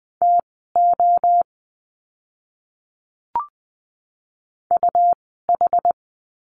Morse code, heard as a keyed tone, about 20 wpm, spaced as for 15 wpm, 700 Hz. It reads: TO UH
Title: 101 Lesson 17f - Letter u - Words 20wpm